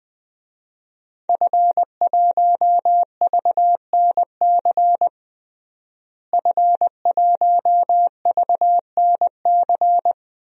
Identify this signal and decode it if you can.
Morse code, heard as a keyed tone, about 21 wpm, 700 Hz. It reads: F1VNC F1VNC
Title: DXCC-100 Call Signs - 5 Characters - Repeated 3x - 20wpm